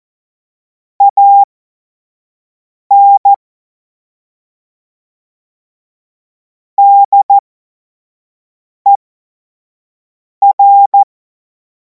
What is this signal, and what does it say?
Morse code, heard as a keyed tone, about 14 wpm, spaced as for 5 wpm, 800 Hz. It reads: AN DER
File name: Morsekurs / Witze / Witz 23. Juni 2026 25 F70